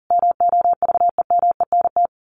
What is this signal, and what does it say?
Morse code, heard as a keyed tone, about 40 wpm, 700 Hz. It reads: MOVEMENT
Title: Sets of 5 Words Encore2 40wpm